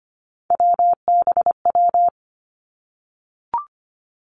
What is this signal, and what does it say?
Morse code, heard as a keyed tone, about 25 wpm, 700 Hz. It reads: WBW
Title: Three Challenging Letter-Number Combinations 25wpm